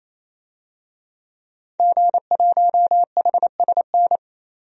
Morse code, heard as a keyed tone, about 27 wpm, 700 Hz. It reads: G1HSN
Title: DXCC-100 Call Signs - 5 Characters - 28wpm